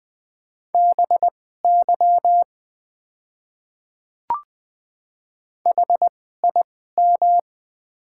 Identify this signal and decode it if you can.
Morse code, heard as a keyed tone, about 20 wpm, spaced as for 14 wpm, 700 Hz. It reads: BY HIM
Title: 2x Character Spacing - Top 100 Words - 20wpm